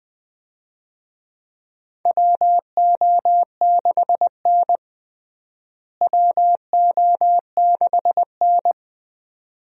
Morse code, heard as a keyed tone, about 20 wpm, 700 Hz. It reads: WO6N WO6N
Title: USA Call Signs - 4 Characters - Repeated 2x - 20wpm